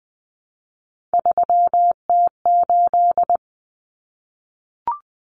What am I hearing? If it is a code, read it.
Morse code, heard as a keyed tone, about 20 wpm, 700 Hz. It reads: 3T8
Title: Common Cut Numbers - 3-Digit 20wpm